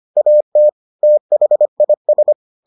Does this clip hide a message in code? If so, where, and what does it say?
Yes: Morse code from 0.2 to 2.3 s, heard as a keyed tone, about 25 wpm, 600 Hz. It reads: AT THIS